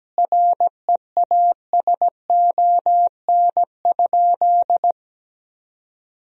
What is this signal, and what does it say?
Morse code, heard as a keyed tone, about 17 wpm, 700 Hz. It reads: REASON?
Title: Sentences from Top 1000 Words - Review Part 1 - 17wpm